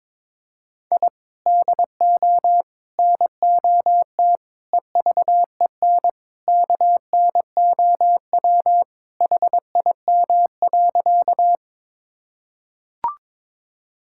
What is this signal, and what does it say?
Morse code, heard as a keyed tone, about 22 wpm, 700 Hz. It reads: I DO NOT EVEN KNOW HIM.
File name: practice / 22wpm / Sentences with 4-Word Phrases - 22wpm